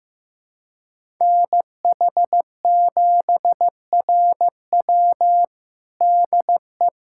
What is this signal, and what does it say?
Morse code, heard as a keyed tone, about 15 wpm, 700 Hz. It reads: NH7RW DE
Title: USA Callsign DE Callsign 15wpm